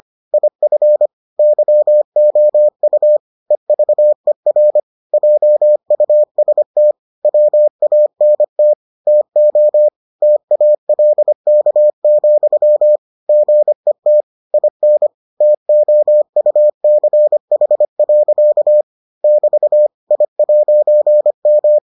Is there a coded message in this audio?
Yes: Morse code from 0.3 to 21.8 s, heard as a keyed tone, about 25 wpm, 600 Hz. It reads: IF YOU EVER JUST WANT TO TALK, GET IN TOUCH. <BT> I'M